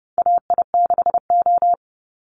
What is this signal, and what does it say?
Morse code, heard as a keyed tone, about 30 wpm, 700 Hz. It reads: AI6O
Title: Call Signs Repeated 3x - 30wpm